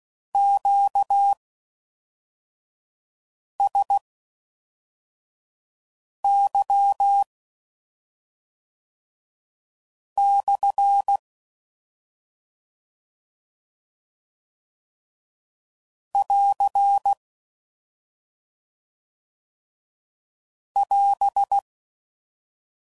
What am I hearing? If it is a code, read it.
Morse code, heard as a keyed tone, about 16 wpm, spaced as for 3 wpm, 800 Hz. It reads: QSY/ <AR> <AS>